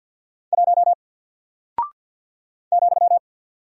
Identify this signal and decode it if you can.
Morse code, heard as a keyed tone, about 50 wpm, 700 Hz. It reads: 1 ,
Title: Single Character (Letter:Number:Punctuation:Prosign) 50wpm